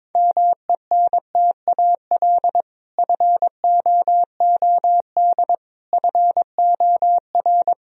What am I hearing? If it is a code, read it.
Morse code, heard as a keyed tone, about 21 wpm, 700 Hz. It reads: MENTAL FOOD FOR
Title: Sentences from Top 1000 Words - Review Part 4 - 22wpm